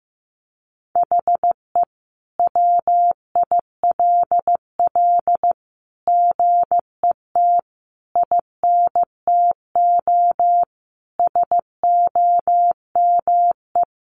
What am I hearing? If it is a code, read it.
Morse code, heard as a keyed tone, about 15 wpm, 700 Hz. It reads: HE WILL GET INTO SOME
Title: Sentences from Top 1500 Words 15wpm